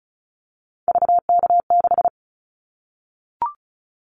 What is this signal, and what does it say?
Morse code, heard as a keyed tone, about 36 wpm, 700 Hz. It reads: VK6
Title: Callsign Prefixes for Top 100 DXCC - 35wpm